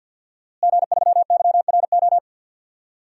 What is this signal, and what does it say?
Morse code, heard as a keyed tone, about 47 wpm, 700 Hz. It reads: G3YRO
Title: DXCC-100 Call Signs - 5 Characters - Repeated 3x - 50wpm